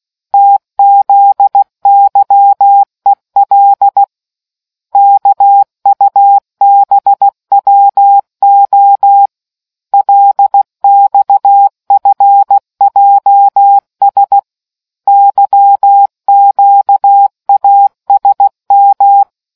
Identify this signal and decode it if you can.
Morse code, heard as a keyed tone, about 16 wpm, 800 Hz. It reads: TZYEL KUBWO LXFJS YQASM